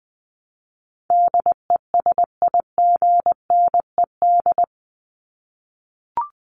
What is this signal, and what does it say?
Morse code, heard as a keyed tone, about 20 wpm, 700 Hz. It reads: DESIGNED
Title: ED Suffix - Top 100 Words with Base Words 20wpm